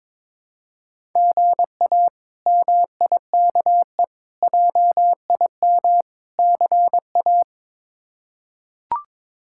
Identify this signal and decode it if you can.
Morse code, heard as a keyed tone, about 22 wpm, 700 Hz. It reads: GA MIKE JIM CA
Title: K1USN SST QSO Practice - 22wpm